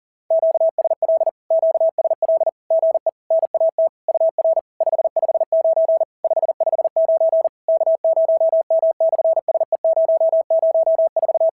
Morse code, heard as a keyed tone, about 40 wpm, 690 Hz. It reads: QSL QSL GE NAT UR 559 559 K0M/SE004
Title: SOTA QSO Practice 40wpm